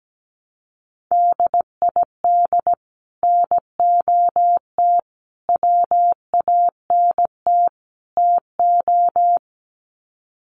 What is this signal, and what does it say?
Morse code, heard as a keyed tone, about 17 wpm, 700 Hz. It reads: DID NOT WANT TO
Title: Sets of 4 Words 17wpm